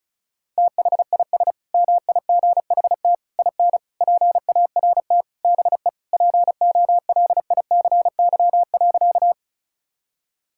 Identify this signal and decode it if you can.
Morse code, heard as a keyed tone, about 34 wpm, 700 Hz. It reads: THIS MIGHT IN PART BE POLICY.
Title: Sentences from Top 1000 Words - Review Part 5 - 35wpm